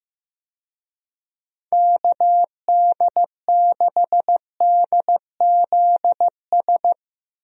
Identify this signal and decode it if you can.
Morse code, heard as a keyed tone, about 15 wpm, 700 Hz. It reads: KD6DZS
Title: USA Call Signs - 6 Characters - 15wpm